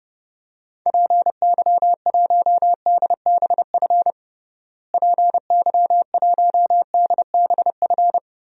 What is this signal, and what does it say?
Morse code, heard as a keyed tone, about 32 wpm, 700 Hz. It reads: PY1DBF PY1DBF
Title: DXCC-100 Call Signs - 6 Characters - Repeated 2x - 30wpm